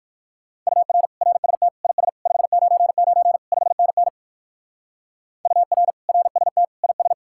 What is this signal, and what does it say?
Morse code, heard as a keyed tone, about 58 wpm, 700 Hz. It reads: UR RST IS 599 5NN UR RST IS